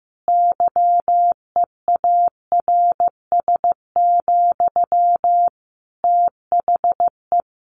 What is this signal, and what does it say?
Morse code, heard as a keyed tone, about 15 wpm, 700 Hz. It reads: YEARS, THE
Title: A Sound Of Thunder 15wpm